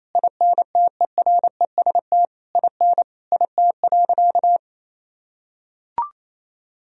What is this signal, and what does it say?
Morse code, heard as a keyed tone, about 28 wpm, 700 Hz. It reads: INTEREST IN IT.
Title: Sentences from Top 1000 Words - Review Part 4 - 28wpm